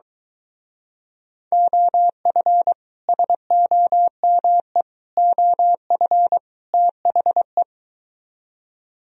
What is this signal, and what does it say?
Morse code, heard as a keyed tone, about 24 wpm, 700 Hz. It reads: OF SOME OF THE